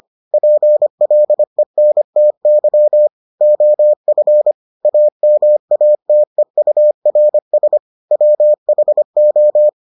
Morse code, heard as a keyed tone, about 25 wpm, 600 Hz. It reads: PLENTY OF AMATEURS WHO